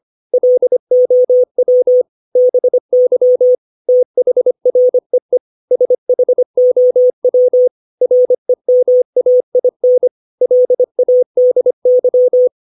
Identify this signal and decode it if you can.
Morse code, heard as a keyed tone, about 25 wpm, 500 Hz. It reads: LOW BY THREE SHOW REMAIN LADY
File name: 632 words / 632 Words 25WPM